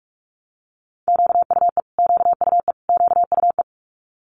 Morse code, heard as a keyed tone, about 46 wpm, 700 Hz. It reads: QUE QUE QUE
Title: Multi-letter Phonemes Repeated 3x 45wpm